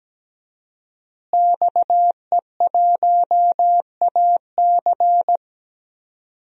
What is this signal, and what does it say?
Morse code, heard as a keyed tone, about 17 wpm, 700 Hz. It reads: XE1AC
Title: DXCC-100 Call Signs - 5 Characters - Repeated 3x - 17wpm